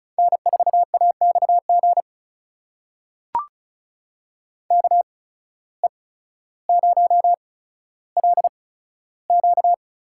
Morse code, heard as a keyed tone, about 34 wpm, spaced as for 15 wpm, 700 Hz. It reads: N4AXG KE0LQ
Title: USA Call Signs - 5 Characters - 8x Character Spacing - 35wpm